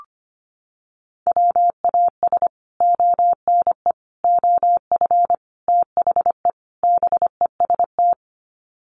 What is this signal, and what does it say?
Morse code, heard as a keyed tone, about 25 wpm, 700 Hz. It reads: WAS ONE OF THE BEST
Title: Sets of 5 Words Encore2 25wpm